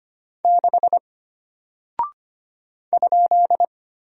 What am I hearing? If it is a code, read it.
Morse code, heard as a keyed tone, about 25 wpm, 700 Hz. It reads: 6 ?
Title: Single Character (Letter:Number:Punctuation:Prosign) Alt 25wpm